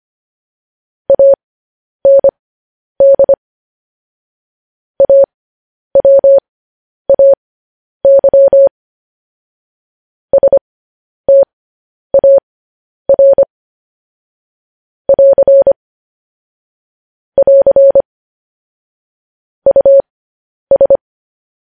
Morse code, heard as a keyed tone, about 25 wpm, spaced as for 10 wpm, 550 Hz. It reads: AND AWAY STAR <AR> <AR> US
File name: podcast / News headlines-1x-25x10